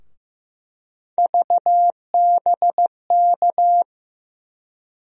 Morse code, heard as a keyed tone, about 15 wpm, 700 Hz. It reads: VBK
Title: USA Call Sign Postfixes - 3 Letters 15wpm